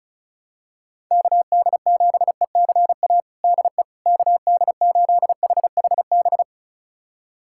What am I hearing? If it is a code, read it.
Morse code, heard as a keyed tone, about 36 wpm, 700 Hz. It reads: KD7ECA DE KD8HHB
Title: USA Callsign DE Callsign - 6 Characters - 35wpm